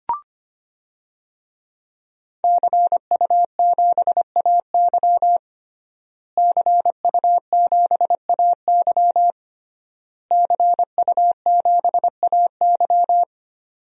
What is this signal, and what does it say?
Morse code, heard as a keyed tone, about 25 wpm, 700 Hz. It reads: CU7AY CU7AY CU7AY
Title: DXCC-100 Call Signs - 5 Characters - Repeated 3x - 25wpm